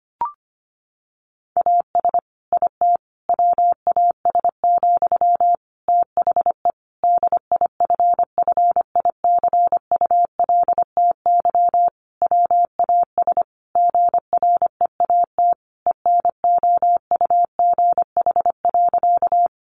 Morse code, heard as a keyed tone, about 25 wpm, 700 Hz. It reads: AS IT WAS, THE DIFFICULTY WAS GREAT ENOUGH.
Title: Sentences from Top 1500 Words 25wpm